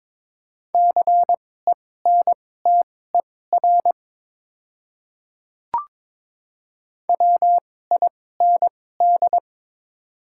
Morse code, heard as a keyed tone, about 22 wpm, spaced as for 16 wpm, 700 Hz. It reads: CENTER WIND
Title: 2x Character Spacing - Top 500 Words - 22wpm